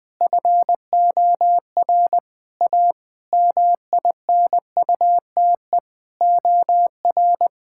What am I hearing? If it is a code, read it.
Morse code, heard as a keyed tone, about 20 wpm, 700 Hz. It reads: FOR A MINUTE OR
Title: Sentences from Top 1600 Words 20wpm